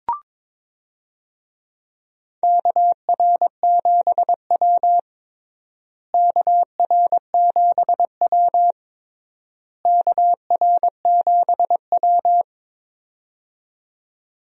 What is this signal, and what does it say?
Morse code, heard as a keyed tone, about 22 wpm, 700 Hz. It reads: KR7W KR7W KR7W
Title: USA Call Signs - 4 Characters - Repeated 3x - 22wpm